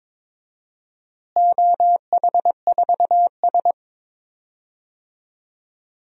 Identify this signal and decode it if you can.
Morse code, heard as a keyed tone, about 21 wpm, 700 Hz. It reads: OH4S
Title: DXCC-100 Call Signs - 4 Characters - Repeated 3x - 22wpm